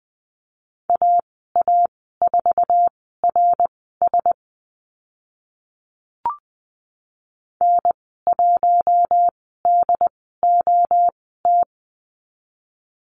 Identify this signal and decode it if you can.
Morse code, heard as a keyed tone, about 20 wpm, spaced as for 14 wpm, 700 Hz. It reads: AA4RS N1DOT
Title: 082 Lesson 14g - Number 2 - Call Signs 20wpm